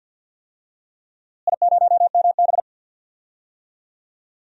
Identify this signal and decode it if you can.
Morse code, heard as a keyed tone, about 50 wpm, 700 Hz. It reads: I0MB
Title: DXCC-100 Call Signs - 4 Characters - Repeated 3x - 50wpm